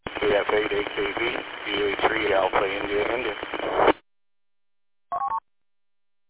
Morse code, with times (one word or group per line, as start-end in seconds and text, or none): none